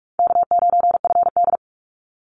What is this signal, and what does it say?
Morse code, heard as a keyed tone, about 48 wpm, 700 Hz. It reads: K9FD
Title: Easy Call Signs - 45wpm